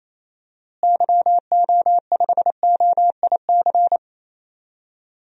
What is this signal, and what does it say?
Morse code, heard as a keyed tone, about 28 wpm, 700 Hz. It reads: YO5OIC
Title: DXCC-100 Call Signs - 6 Characters - Repeated 2x - 28wpm